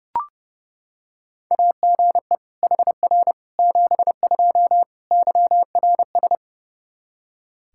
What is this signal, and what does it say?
Morse code, heard as a keyed tone, about 32 wpm, 700 Hz. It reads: AGE HR 72 YRS
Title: K1USN SST QSO Practice 30wpm